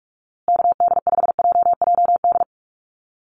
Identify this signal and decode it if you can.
Morse code, heard as a keyed tone, about 44 wpm, 700 Hz. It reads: KD5JJD
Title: USA Call Signs Repeated 3x 45wpm